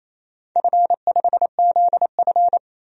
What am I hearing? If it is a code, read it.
Morse code, heard as a keyed tone, about 28 wpm, 700 Hz. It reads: F5ZF
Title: DXCC-100 Call Signs - 4 Characters - 28wpm